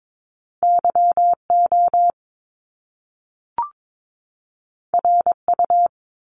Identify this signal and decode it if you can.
Morse code, heard as a keyed tone, about 22 wpm, 700 Hz. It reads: YO RU
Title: Top Two Letter Combinations in Words 22wpm